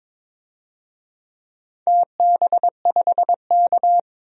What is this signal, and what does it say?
Morse code, heard as a keyed tone, about 23 wpm, 700 Hz. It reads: TB5K
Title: DXCC-100 Call Signs - 4 Characters - 22wpm